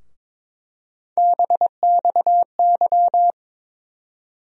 Morse code, heard as a keyed tone, about 22 wpm, 700 Hz. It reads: BXY
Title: USA Call Sign Postfixes - 3 Letters 22wpm